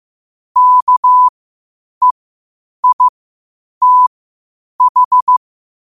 Morse code, heard as a keyed tone, about 15 wpm, spaced as for 8 wpm, 1000 Hz. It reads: KEITH
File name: qotdfiles / 8wpm